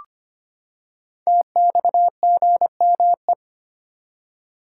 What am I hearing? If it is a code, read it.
Morse code, heard as a keyed tone, about 25 wpm, 700 Hz. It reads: TXGME
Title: Code Groups - 5 Letter:Numbers - 25wpm